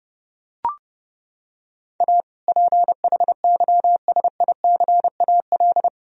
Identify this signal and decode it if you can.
Morse code, heard as a keyed tone, about 28 wpm, 700 Hz. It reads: A PHYSICAL
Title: Sentences from Top 1000 Words - Review Part 4 - 30wpm